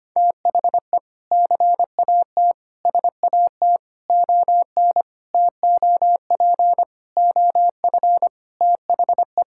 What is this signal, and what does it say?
Morse code, heard as a keyed tone, about 25 wpm, 700 Hz. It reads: THE CAT SAT ON TOP OF THE
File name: practice / 25wpm / Sentences with 4-Word Phrases - 25wpm